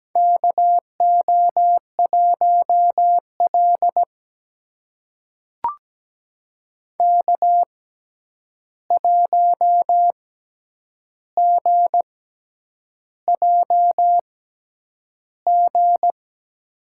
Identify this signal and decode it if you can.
Morse code, heard as a keyed tone, about 17 wpm, spaced as for 8 wpm, 700 Hz. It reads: KO1L K1GJG
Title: USA Call Signs - 6x Character Spacing - 17wpm